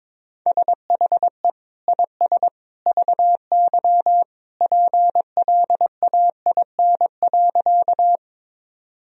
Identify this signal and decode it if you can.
Morse code, heard as a keyed tone, about 22 wpm, 700 Hz. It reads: SHE IS VY PLAIN.
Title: Sentences from Top 400 Words and Abbreviations 22wpm